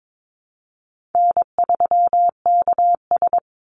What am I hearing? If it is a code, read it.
Morse code, heard as a keyed tone, about 22 wpm, 700 Hz. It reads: N3KS
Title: Easy Call Signs Repeated 2x - 22wpm